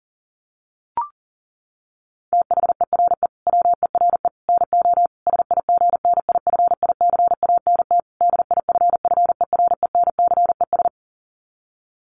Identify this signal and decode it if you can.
Morse code, heard as a keyed tone, about 40 wpm, 700 Hz. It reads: THERE WERE NO SIGNIFICANT DIFFERENCES
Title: Sets of 5 Words Encore2 40wpm